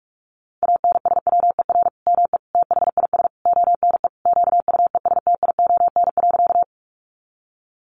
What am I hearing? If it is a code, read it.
Morse code, heard as a keyed tone, about 44 wpm, 700 Hz. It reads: ANSWER ME THIS ONE QUESTION.
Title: Sentences from Top 400 Words 45wpm